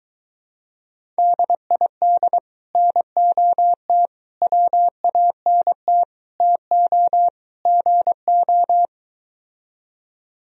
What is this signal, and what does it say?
Morse code, heard as a keyed tone, about 23 wpm, 700 Hz. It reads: DID NOT WANT TO GO